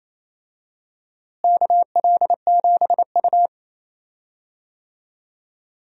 Morse code, heard as a keyed tone, about 28 wpm, 700 Hz. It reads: KL7U
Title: DXCC-100 Call Signs - 4 Characters - Repeated 2x - 28wpm